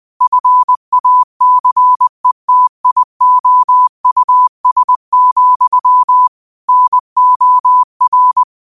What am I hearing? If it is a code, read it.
Morse code, heard as a keyed tone, about 20 wpm, 1000 Hz. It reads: FACETIOUS, NOR
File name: qotdfiles / 20wpm